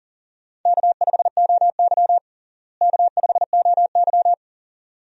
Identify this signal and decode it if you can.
Morse code, heard as a keyed tone, about 40 wpm, 700 Hz. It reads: K5OY K5OY
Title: USA Call Signs - 4 Characters - Repeated 2x - 40wpm